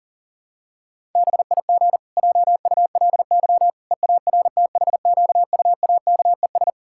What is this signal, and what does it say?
Morse code, heard as a keyed tone, about 40 wpm, 700 Hz. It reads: BIG JULY EARTHQUAKES
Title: Pangram Challenge v2 - 40wpm